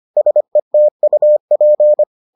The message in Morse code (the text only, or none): SETUP